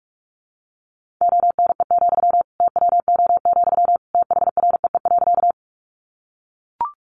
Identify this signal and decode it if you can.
Morse code, heard as a keyed tone, about 44 wpm, 700 Hz. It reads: ONE, TWO, THREE.
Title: Sentences from Top 200 Words 45wpm